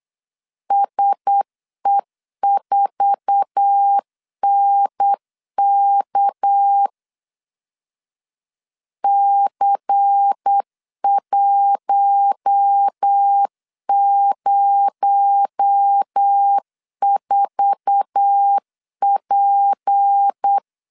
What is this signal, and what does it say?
Morse code, heard as a keyed tone, about 8 wpm, 800 Hz. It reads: SE4NK C104P
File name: LEZIONI / 8 YJ45 / 8 YJ4 8wpm 3246